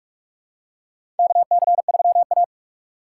Morse code, heard as a keyed tone, about 46 wpm, 700 Hz. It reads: KC3A
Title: USA Call Signs - 4 Characters - 2x Character Spacing - 45wpm